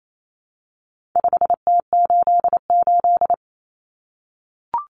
Morse code, heard as a keyed tone, about 28 wpm, 700 Hz. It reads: 5T88
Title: Common Cut Numbers - 4-Digit 28wpm